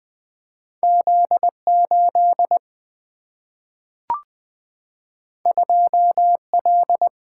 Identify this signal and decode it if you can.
Morse code, heard as a keyed tone, about 20 wpm, 700 Hz. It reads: Z8 2L
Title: Single-Letter Single-Number 20wpm